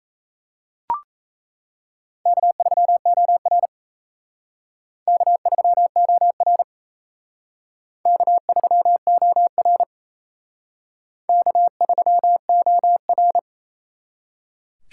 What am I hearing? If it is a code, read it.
Morse code, heard as a keyed tone, about 35 wpm, 700 Hz. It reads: K3OR K3OR K3OR K3OR
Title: Speed Racing - USA 1x2 Call Signs 28wpm